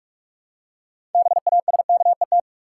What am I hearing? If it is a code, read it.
Morse code, heard as a keyed tone, about 44 wpm, 700 Hz. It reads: BASKET
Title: Six Letter Words - 45wpm